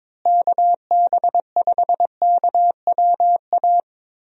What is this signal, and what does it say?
Morse code, heard as a keyed tone, about 22 wpm, 700 Hz. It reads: KB5KWA
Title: USA Call Signs - 6 Characters - Repeated 3x - 22wpm